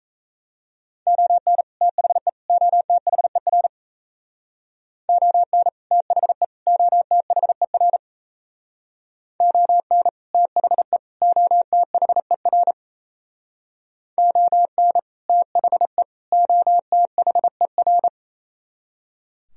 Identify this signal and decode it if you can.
Morse code, heard as a keyed tone, about 36 wpm, 700 Hz. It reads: ON THE OTHER ON THE OTHER ON THE OTHER ON THE OTHER